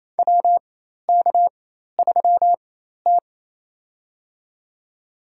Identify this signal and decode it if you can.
Morse code, heard as a keyed tone, about 28 wpm, spaced as for 13 wpm, 700 Hz. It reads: WK3T
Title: USA Call Signs - 4 Characters - 4x Character Spacing - 28wpm